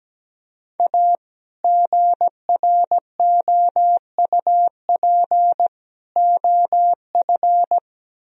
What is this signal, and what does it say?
Morse code, heard as a keyed tone, about 17 wpm, 700 Hz. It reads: A GROUP OF
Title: Speed-Racing - Sets of 3 Words 17wpm